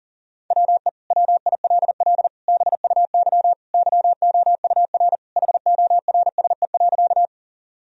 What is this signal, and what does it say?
Morse code, heard as a keyed tone, about 40 wpm, 700 Hz. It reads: WE WILL BUY YOUR HORSE.
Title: Sentences from Top 1000 Words - Review Part 4 - 40wpm